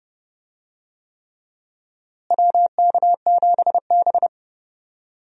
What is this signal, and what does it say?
Morse code, heard as a keyed tone, about 30 wpm, 700 Hz. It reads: WK7B
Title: USA Call Signs - 4 Characters - Repeated 3x - 30wpm